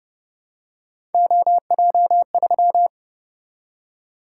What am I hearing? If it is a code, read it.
Morse code, heard as a keyed tone, about 30 wpm, 700 Hz. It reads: OJ3